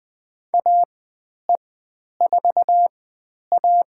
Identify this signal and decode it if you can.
Morse code, heard as a keyed tone, about 20 wpm, spaced as for 10 wpm, 700 Hz. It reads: AE4A
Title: 109 Lesson 18g - Letter m - Call Signs 20wpm